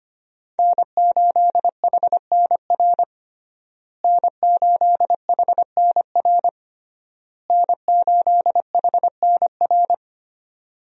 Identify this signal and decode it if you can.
Morse code, heard as a keyed tone, about 25 wpm, 700 Hz. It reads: N8HNR N8HNR N8HNR
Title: USA Call Signs - 5 Characters - Repeated 3x - 25wpm